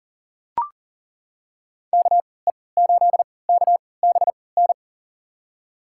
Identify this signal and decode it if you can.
Morse code, heard as a keyed tone, about 38 wpm, spaced as for 22 wpm, 700 Hz. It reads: KE8KBN
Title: USA Call Signs - 6 Characters - 3x Character Spacing - 40wpm